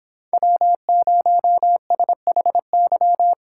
Morse code, heard as a keyed tone, about 25 wpm, 700 Hz. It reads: W0SHY